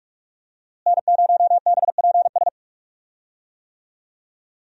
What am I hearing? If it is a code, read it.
Morse code, heard as a keyed tone, about 46 wpm, 700 Hz. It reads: N0BPS
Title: USA Call Signs - 5 Characters - Repeated 2x - 45wpm